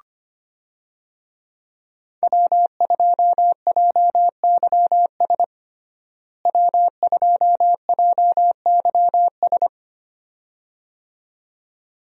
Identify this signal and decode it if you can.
Morse code, heard as a keyed tone, about 25 wpm, 700 Hz. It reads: W2JYS W2JYS
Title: USA Call Signs - 5 Characters - Repeated 2x - 25wpm